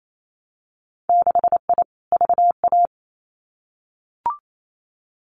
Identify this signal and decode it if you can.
Morse code, heard as a keyed tone, about 28 wpm, 700 Hz. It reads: 6I VA
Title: Winter Field Day Exchanges Encore 28wpm